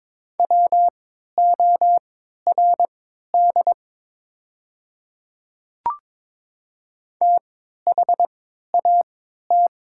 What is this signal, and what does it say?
Morse code, heard as a keyed tone, about 23 wpm, spaced as for 13 wpm, 700 Hz. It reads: WORD THAT
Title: 3x Character Spacing - Top 100 Words - 22wpm